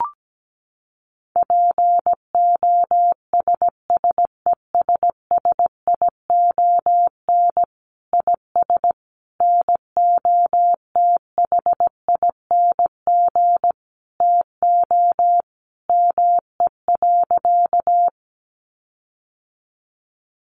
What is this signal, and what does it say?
Morse code, heard as a keyed tone, about 17 wpm, 700 Hz. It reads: POSSESSION IS NOTHING TO ME.
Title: Sentences from Top 1400 Words 17wpm